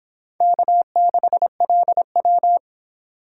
Morse code, heard as a keyed tone, about 26 wpm, 700 Hz. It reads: K6LW